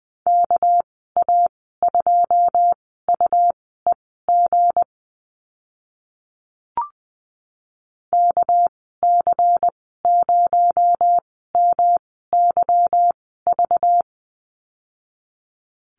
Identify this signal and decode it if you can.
Morse code, heard as a keyed tone, about 20 wpm, spaced as for 14 wpm, 700 Hz. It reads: KA2UEG KC0MYV
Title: 238 Lesson 38g - Letter q - Call Signs 20wpm